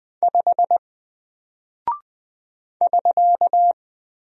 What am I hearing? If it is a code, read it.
Morse code, heard as a keyed tone, about 20 wpm, 700 Hz. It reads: 5 <SK>
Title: Single Character (Letter:Number:Punctuation:Prosign) Alt 20wpm